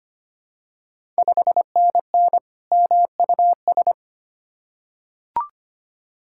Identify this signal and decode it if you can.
Morse code, heard as a keyed tone, about 25 wpm, 700 Hz. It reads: 5NN MUS
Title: Ontario QSO Party Exchange 25wpm